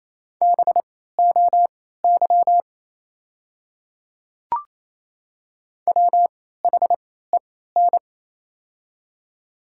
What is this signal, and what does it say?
Morse code, heard as a keyed tone, about 28 wpm, spaced as for 16 wpm, 700 Hz. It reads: BOY WHEN
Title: 3x Character Spacing - Top 300 Words - 28wpm